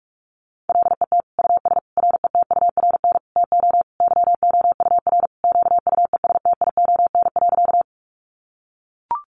Morse code, heard as a keyed tone, about 44 wpm, 700 Hz. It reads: LET US RETURN TO YOUR QUESTION.
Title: Sentences from Top 800 Words 45wpm